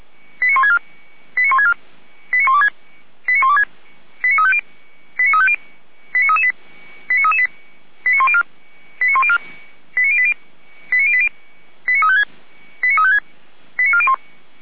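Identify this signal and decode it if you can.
Five-tone selective call (ZVEI), heard as quick runs of five tones, tones of about 70 ms. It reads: 89164 89164 89126 89126 89350 89350 89338 89338 89104 89104 89080 89080 89356 89401
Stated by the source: ZVEI five-tone selective call received by SDR.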